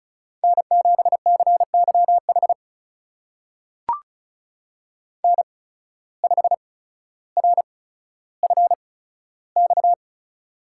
Morse code, heard as a keyed tone, about 36 wpm, spaced as for 15 wpm, 700 Hz. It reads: N7CYH N5RFX
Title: USA Call Signs - 5 Characters - 8x Character Spacing - 35wpm